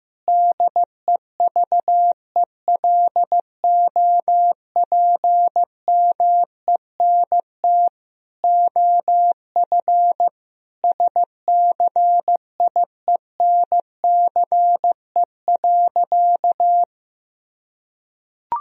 Morse code, heard as a keyed tone, about 16 wpm, 700 Hz. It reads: DEVELOPMENT OF SCIENCE.
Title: Sentences from Top 1000 Words - Review Part 4 - 15wpm